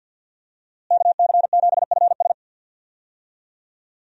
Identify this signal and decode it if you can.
Morse code, heard as a keyed tone, about 50 wpm, 700 Hz. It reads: KC7FS
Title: USA Call Signs - 5 Characters - Repeated 3x - 50wpm